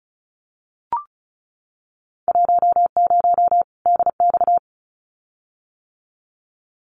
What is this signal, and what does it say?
Morse code, heard as a keyed tone, about 34 wpm, 700 Hz. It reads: 10 DX
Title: Winter Field Day Exchanges 35wpm